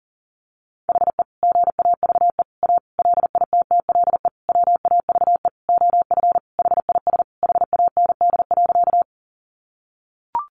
Morse code, heard as a keyed tone, about 40 wpm, 700 Hz. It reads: HE GAVE A LITTLE WAVE OF HIS HAND.
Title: Sentences from Top 1400 Words 40wpm